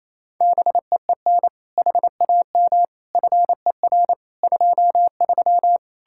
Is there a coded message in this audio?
Yes: Morse code from 0.4 to 5.8 s, heard as a keyed tone, about 36 wpm, 700 Hz. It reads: BEEN HAM FER 23